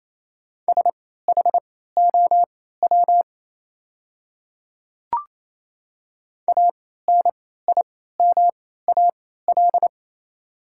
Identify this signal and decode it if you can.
Morse code, heard as a keyed tone, about 27 wpm, spaced as for 16 wpm, 690 Hz. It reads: SHOW ANIMAL
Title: 3x Character Spacing - Top 200 Words - 28wpm